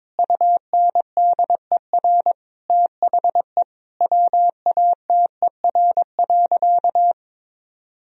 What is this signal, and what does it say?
Morse code, heard as a keyed tone, about 22 wpm, 700 Hz. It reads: UNDER THE WATER.
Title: Sentences from Top 1000 Words - Review Part 1 - 22wpm